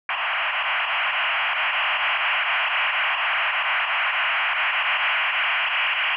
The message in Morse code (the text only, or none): none